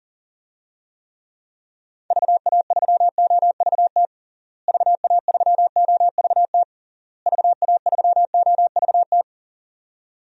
Morse code, heard as a keyed tone, about 40 wpm, 700 Hz. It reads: VA3OVT VA3OVT VA3OVT
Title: DXCC-100 Call Signs - 6 Characters - Repeated 3x - 40wpm